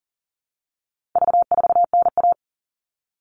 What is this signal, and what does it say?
Morse code, heard as a keyed tone, about 38 wpm, 700 Hz. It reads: V4NA
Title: All Cut Numbers - 4-Digit 40wpm